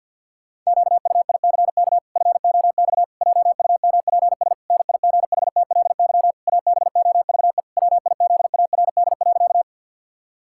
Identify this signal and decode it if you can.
Morse code, heard as a keyed tone, about 46 wpm, 700 Hz. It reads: QUICK FOX JUMPS NIGHTLY ABOVE WIZARD.
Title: Pangram Challenge v2 - 50wpm